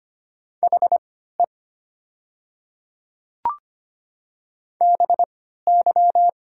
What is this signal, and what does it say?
Morse code, heard as a keyed tone, about 25 wpm, spaced as for 14 wpm, 700 Hz. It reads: HE BY
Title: 3x Character Spacing - Top 200 Words - 25wpm